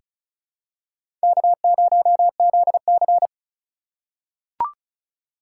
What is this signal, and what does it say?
Morse code, heard as a keyed tone, about 34 wpm, 700 Hz. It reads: K0ZC
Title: USA Call Signs - 4 Characters - 2x Character Spacing - 35wpm